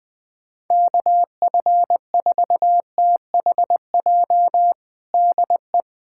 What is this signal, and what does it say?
Morse code, heard as a keyed tone, about 20 wpm, 700 Hz. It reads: KF4THJ DE
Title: USA Callsign DE Callsign 20wpm